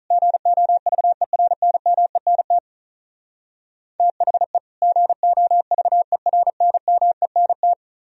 Morse code, heard as a keyed tone, about 43 wpm, 700 Hz. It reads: GOVERNMENT THE GOVERNMENT
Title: Speed-Racing - Sets of 2 Words 35wpm